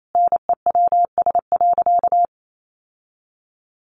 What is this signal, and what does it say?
Morse code, heard as a keyed tone, about 28 wpm, 700 Hz. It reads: NEWS.
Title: Sentences from Top 800 Words 28wpm